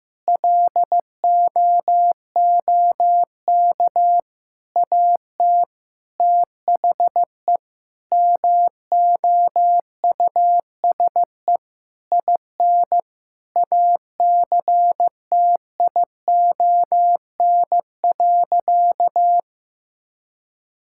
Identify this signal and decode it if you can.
Morse code, heard as a keyed tone, about 15 wpm, 700 Hz. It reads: LOOK AT THE MOUSE IN ACTION.
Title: Sentences from Top 1000 Words - Review Part 5 - 15wpm